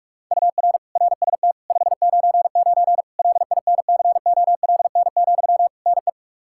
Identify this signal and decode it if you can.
Morse code, heard as a keyed tone, about 46 wpm, 700 Hz. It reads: UR RST 599 LINCOLN, NE
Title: Practice QSO - Exchange 1 - Beginning - 45wpm